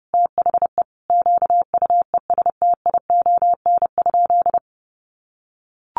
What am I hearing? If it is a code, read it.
Morse code, heard as a keyed tone, about 30 wpm, 700 Hz. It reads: THE QUESTION?
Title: Sentences from Top 600 Words 30wpm